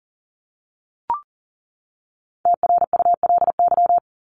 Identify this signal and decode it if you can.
Morse code, heard as a keyed tone, about 40 wpm, 700 Hz. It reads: TRULY
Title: LY Suffix - Top 100 Words with Base Words 40wpm